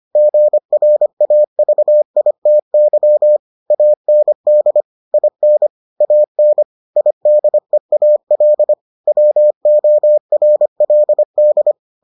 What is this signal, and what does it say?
Morse code, heard as a keyed tone, about 25 wpm, 600 Hz. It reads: GRAVITY AND IN AN IDEAL WORLD